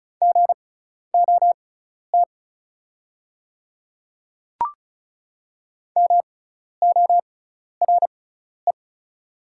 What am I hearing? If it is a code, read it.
Morse code, heard as a keyed tone, about 36 wpm, spaced as for 12 wpm, 700 Hz. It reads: GOT MORE
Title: 6x Character Spacing - Top 300 Words - 35wpm